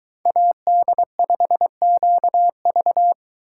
Morse code, heard as a keyed tone, about 22 wpm, 700 Hz. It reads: AD5QV